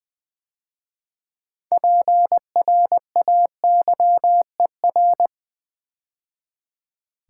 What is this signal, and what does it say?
Morse code, heard as a keyed tone, about 20 wpm, 700 Hz. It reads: PRAYER
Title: 6 Letter Words - First 3 Letters Given - 20wpm